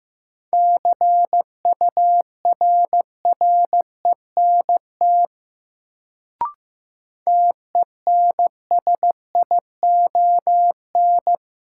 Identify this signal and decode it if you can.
Morse code, heard as a keyed tone, about 15 wpm, 700 Hz. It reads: CURRENT TENSION
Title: Sentences from Top 1600 Words - New Word Drill Alt - 15wpm